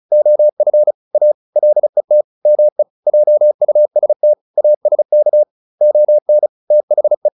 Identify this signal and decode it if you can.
Morse code, heard as a keyed tone, about 34 wpm, 600 Hz. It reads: OF A LET ME JUST ASK ON THE